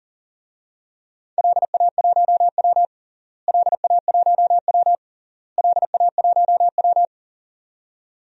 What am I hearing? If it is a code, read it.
Morse code, heard as a keyed tone, about 40 wpm, 700 Hz. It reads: LA1W LA1W LA1W
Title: DXCC-100 Call Signs - 4 Characters - Repeated 3x - 40wpm